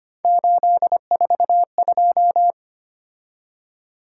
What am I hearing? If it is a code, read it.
Morse code, heard as a keyed tone, about 25 wpm, 700 Hz. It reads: 842
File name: Three Letter-Number Combinations - 25wpm